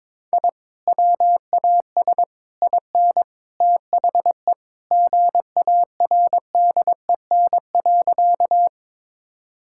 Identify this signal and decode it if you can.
Morse code, heard as a keyed tone, about 21 wpm, 700 Hz. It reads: I WAS IN THE GARDEN.
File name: practice / 22wpm / Sentences with 3-Word Phrases - 22wpm